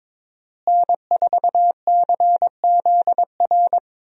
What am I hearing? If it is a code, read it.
Morse code, heard as a keyed tone, about 22 wpm, 700 Hz. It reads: N4CZR
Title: USA Call Signs - 5 Characters - Repeated 2x - 22wpm